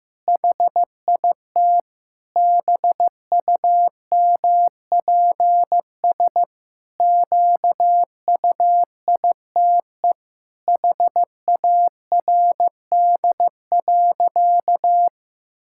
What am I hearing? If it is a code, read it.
Morse code, heard as a keyed tone, about 14 wpm, 700 Hz. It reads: HIT BUMPS QUITE HARD.
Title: Pangram Challenge v2 - 15wpm